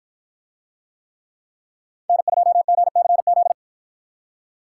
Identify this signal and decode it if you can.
Morse code, heard as a keyed tone, about 53 wpm, 700 Hz. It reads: N2GCZ